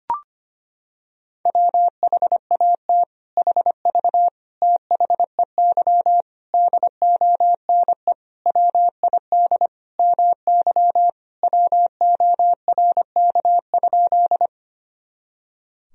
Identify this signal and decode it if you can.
Morse code, heard as a keyed tone, about 25 wpm, 700 Hz. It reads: WHAT HV THEY DONE WID MY WORK?
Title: Sentences from Top 500 Words and Abbreviations 25wpm